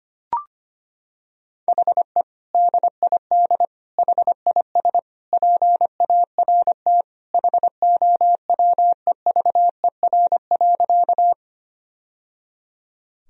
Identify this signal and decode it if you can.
Morse code, heard as a keyed tone, about 25 wpm, 700 Hz. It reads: HE DID HIS PART HOWEVER.
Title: Sentences from Top 1000 Words - Review Part 1 - 25wpm